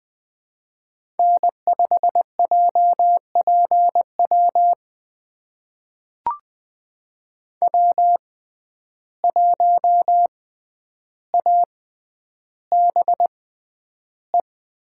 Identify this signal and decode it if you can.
Morse code, heard as a keyed tone, about 20 wpm, spaced as for 10 wpm, 700 Hz. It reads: N5JPW W1ABE
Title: USA Call Signs - 5 Characters - 6x Character Spacing - 20wpm